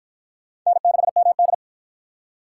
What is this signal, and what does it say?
Morse code, heard as a keyed tone, about 53 wpm, 700 Hz. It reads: N6MD